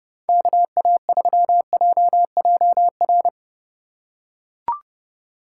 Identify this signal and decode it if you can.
Morse code, heard as a keyed tone, about 32 wpm, 700 Hz. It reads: KA3JJR
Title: USA Call Signs - 6 Characters - 4x Character Spacing - 30wpm